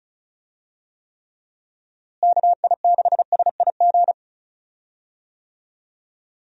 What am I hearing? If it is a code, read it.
Morse code, heard as a keyed tone, about 34 wpm, 700 Hz. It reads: KI6SIG
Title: USA Call Signs - 6 Characters - Repeated 3x - 35wpm